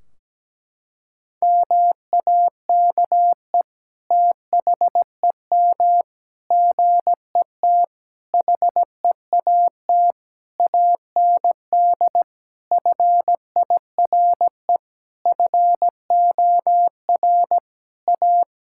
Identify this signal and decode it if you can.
Morse code, heard as a keyed tone, about 17 wpm, 700 Hz. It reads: MAKE THEM GET HEAT AND FIRE FOR A
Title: Sentences from Top 1600 Words 17wpm